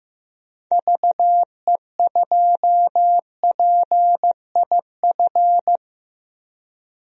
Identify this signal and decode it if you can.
Morse code, heard as a keyed tone, about 15 wpm, 700 Hz. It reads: VE2PIF
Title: DXCC-100 Call Signs - 6 Characters - Repeated 2x - 15wpm